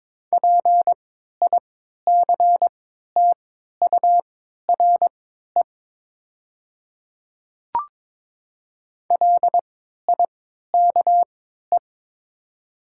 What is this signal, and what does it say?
Morse code, heard as a keyed tone, about 22 wpm, spaced as for 12 wpm, 700 Hz. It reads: PICTURE LIKE
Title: 3x Character Spacing - Top 200 Words - 22wpm